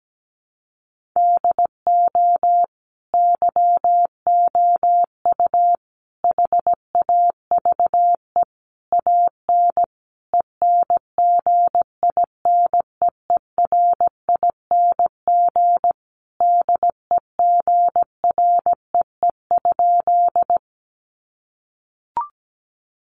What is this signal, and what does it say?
Morse code, heard as a keyed tone, about 17 wpm, 700 Hz. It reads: DO YOU HAVE AN ENGINEERING DEGREE?
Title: Sentences from Top 1500 Words 17wpm